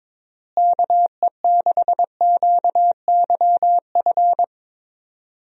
Morse code, heard as a keyed tone, about 22 wpm, 700 Hz. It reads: KE6QYF
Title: USA Call Signs - 6 Characters - Repeated 3x - 22wpm